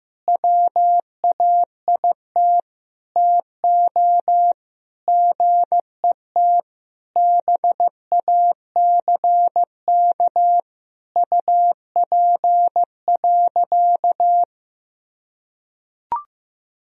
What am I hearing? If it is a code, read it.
Morse code, heard as a keyed tone, about 16 wpm, 700 Hz. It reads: WAIT TO GET BACK UP.
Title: Sentences from Top 1000 Words - Review Part 2 - 15wpm